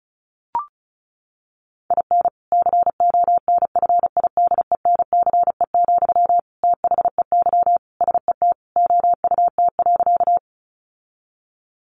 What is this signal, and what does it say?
Morse code, heard as a keyed tone, about 34 wpm, 700 Hz. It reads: IN CONFIDENCE, THEY SET OUT.
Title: Sentences from Top 1500 Words 35wpm